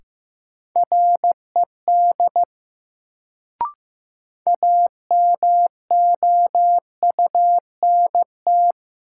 Morse code, heard as a keyed tone, about 15 wpm, 700 Hz. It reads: RED AMOUNT
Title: Sentences from Top 700 Words - New Word Drill - 15wpm